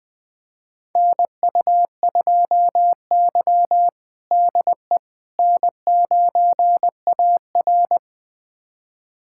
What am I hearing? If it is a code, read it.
Morse code, heard as a keyed tone, about 20 wpm, 700 Hz. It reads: NU2Y DE N9AR
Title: USA Callsign DE Callsign - 4 Characters - 20wpm